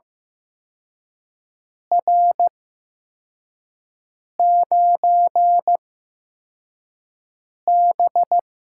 Morse code, heard as a keyed tone, about 15 wpm, spaced as for 4 wpm, 700 Hz. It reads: R9B
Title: USA Call Signs - 4 Characters - 8x Character Spacing - 15wpm